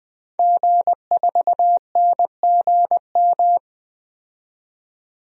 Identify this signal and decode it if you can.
Morse code, heard as a keyed tone, about 21 wpm, 700 Hz. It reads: G4NGM
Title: DXCC-100 Call Signs - 5 Characters - 20wpm